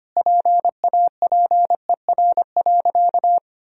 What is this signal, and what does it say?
Morse code, heard as a keyed tone, about 25 wpm, 700 Hz. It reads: PAPER.
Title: Sentences from Top 1000 Words - Review Part 2 - 25wpm